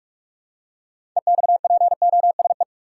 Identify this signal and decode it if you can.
Morse code, heard as a keyed tone, about 46 wpm, 700 Hz. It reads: EXPOSE
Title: Six Letter Words - 45wpm